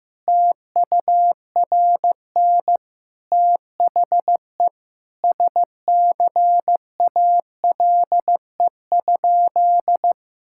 Morse code, heard as a keyed tone, about 15 wpm, 700 Hz. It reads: TURN THE SCALE?
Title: Sentences from Top 1000 Words - Review Part 5 - 15wpm